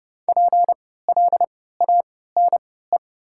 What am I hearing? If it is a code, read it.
Morse code, heard as a keyed tone, about 32 wpm, spaced as for 17 wpm, 700 Hz. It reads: PLANE
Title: 3x Character Spacing - Top 500 Words - 30wpm